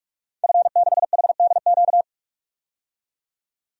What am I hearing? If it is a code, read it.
Morse code, heard as a keyed tone, about 46 wpm, 700 Hz. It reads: F6HDQ
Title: DXCC-100 Call Signs - 5 Characters - 45wpm